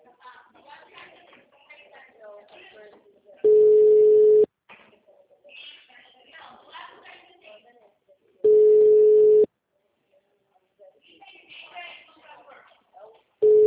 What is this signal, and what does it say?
Ringback tone (Europe, 425 Hz), with speech.